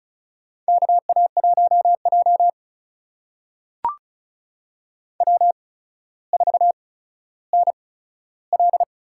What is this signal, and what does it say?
Morse code, heard as a keyed tone, about 36 wpm, spaced as for 15 wpm, 700 Hz. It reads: KA1J W4NL
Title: USA Call Signs - 4 Characters - 8x Character Spacing - 35wpm